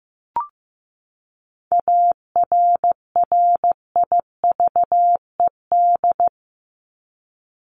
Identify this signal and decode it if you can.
Morse code, heard as a keyed tone, about 15 wpm, 700 Hz. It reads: ARRIVED
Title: ED Suffix - Top 100 Words with Base Words 15wpm